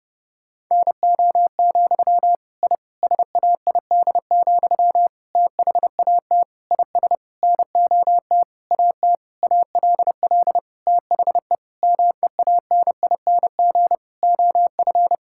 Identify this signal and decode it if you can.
Morse code, heard as a keyed tone, about 30 wpm, 700 Hz. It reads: NO, I SAID, THAT IS NOT AT ALL THE MEANING OF
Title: Sentences from Top 1000 Words - Review Part 6 - 30wpm